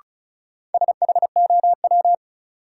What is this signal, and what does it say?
Morse code, heard as a keyed tone, about 36 wpm, 700 Hz. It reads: SHOW
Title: Sentences from Top 200 Words - New Word Drill - 35wpm